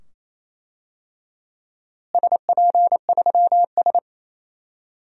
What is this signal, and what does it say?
Morse code, heard as a keyed tone, about 28 wpm, 700 Hz. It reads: SP3S
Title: DXCC-100 Call Signs - 4 Characters - Repeated 2x - 28wpm